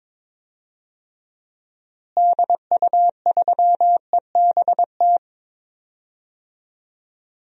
Morse code, heard as a keyed tone, about 22 wpm, 700 Hz. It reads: DU3EBT
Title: DXCC-100 Call Signs - 6 Characters - Repeated 2x - 22wpm